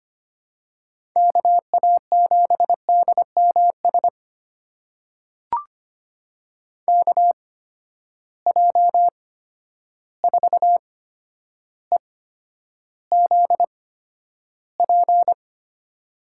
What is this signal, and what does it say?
Morse code, heard as a keyed tone, about 25 wpm, spaced as for 11 wpm, 700 Hz. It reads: KA7DMS KJ4EZP
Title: USA Call Signs - 6 Characters - 8x Character Spacing - 25wpm